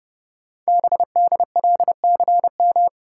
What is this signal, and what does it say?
Morse code, heard as a keyed tone, about 30 wpm, 700 Hz. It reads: BDLCM
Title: Code Groups - 5 Letters - 30wpm